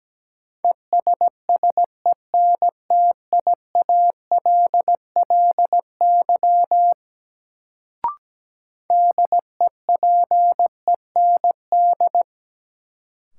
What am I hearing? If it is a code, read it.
Morse code, heard as a keyed tone, about 17 wpm, 700 Hz. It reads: ESSENTIALLY DEPEND
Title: Sentences from Top 1300 Words - New Word Drill Alt - 17wpm